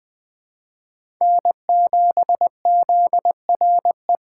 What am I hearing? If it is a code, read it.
Morse code, heard as a keyed tone, about 20 wpm, 700 Hz. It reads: N7ZRE
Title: USA Call Signs - 5 Characters - 3x Character Spacing - 20wpm